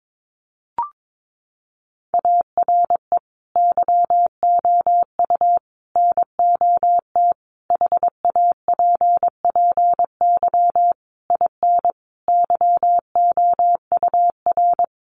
Morse code, heard as a keyed tone, about 22 wpm, 700 Hz. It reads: ARE YOU NOT HAPPY IN YOUR
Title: Sentences from Top 1000 Words 22wpm